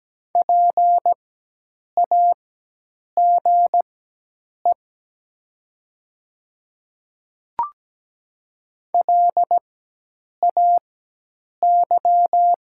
Morse code, heard as a keyed tone, about 17 wpm, spaced as for 8 wpm, 700 Hz. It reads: PAGE LAY
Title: 4x Character Spacing - Top 500 Words - 17wpm